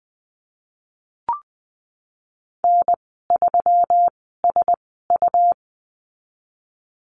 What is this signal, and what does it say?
Morse code, heard as a keyed tone, about 20 wpm, spaced as for 15 wpm, 700 Hz. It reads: N3SU
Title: 167 Lesson 27g - Number 7 - Call Signs 20wpm